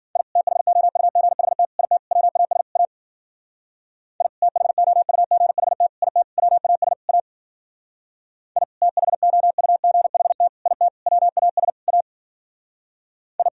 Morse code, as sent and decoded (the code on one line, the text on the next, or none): .. / - .... --- ..- --. .... - / .. - / .-- .- ... / .- / .. / - .... --- ..- --. .... - / .. - / .-- .- ... / .- / .. / - .... --- ..- --. .... - / .. - / .-- .- ... / .- / ..
I THOUGHT IT WAS A I THOUGHT IT WAS A I THOUGHT IT WAS A I